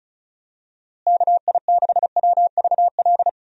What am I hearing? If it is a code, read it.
Morse code, heard as a keyed tone, about 36 wpm, 700 Hz. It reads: KI6WVL
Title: USA Call Signs - 6 Characters - Repeated 3x - 35wpm